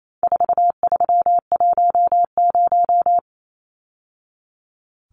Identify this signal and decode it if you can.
Morse code, heard as a keyed tone, about 28 wpm, 700 Hz. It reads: 4310
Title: Numbers - 4-Digit 28wpm